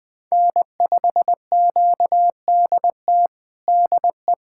Morse code, heard as a keyed tone, about 20 wpm, 700 Hz. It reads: N5QDT DE
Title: USA Callsign DE Callsign - 5 Characters - 20wpm